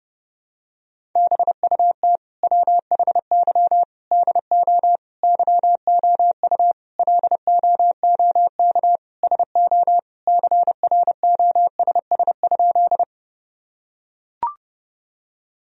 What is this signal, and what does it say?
Morse code, heard as a keyed tone, about 30 wpm, 700 Hz. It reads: BUT WHY DO YOU LOOK SO CROSS?
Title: Sentences from Top 1000 Words - Review Part 1 - 30wpm